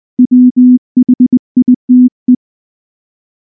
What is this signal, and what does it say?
Morse code, heard as a keyed tone, about 20 wpm, 250 Hz. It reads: WHITE
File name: morse / morse-training-20WPM-6